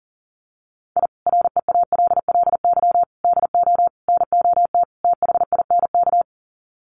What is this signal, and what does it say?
Morse code, heard as a keyed tone, about 40 wpm, 700 Hz. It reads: I REALLY DO NOT THINK
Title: Sets of 5 Words Encore2 40wpm